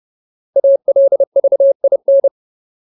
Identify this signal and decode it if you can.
Morse code, heard as a keyed tone, about 30 wpm, 550 Hz. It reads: ALVIN
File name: Ultimate Word List / 30WPM Ultimate